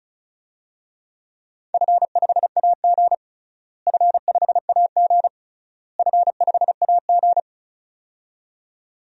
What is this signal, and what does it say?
Morse code, heard as a keyed tone, about 34 wpm, 700 Hz. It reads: F5AG F5AG F5AG
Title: DXCC-100 Call Signs - 4 Characters - Repeated 3x - 35wpm